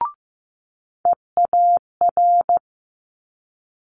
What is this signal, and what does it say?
Morse code, heard as a keyed tone, about 15 wpm, 700 Hz. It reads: EAR
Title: Multi-letter Phonemes Quick Fire 15wpm